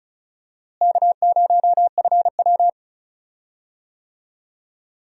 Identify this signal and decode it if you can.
Morse code, heard as a keyed tone, about 36 wpm, 700 Hz. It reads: K0FW
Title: USA Call Signs - 4 Characters - Repeated 2x - 35wpm